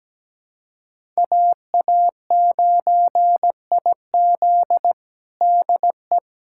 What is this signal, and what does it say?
Morse code, heard as a keyed tone, about 17 wpm, 700 Hz. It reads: AA9IZ DE
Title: USA Callsign DE Callsign - 5 Characters - 17wpm